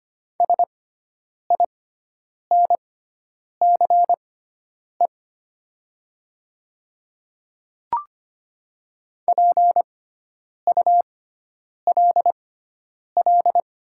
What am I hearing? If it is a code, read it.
Morse code, heard as a keyed tone, about 25 wpm, spaced as for 9 wpm, 690 Hz. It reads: SINCE PULL
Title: 6x Character Spacing - Top 500 Words - 25wpm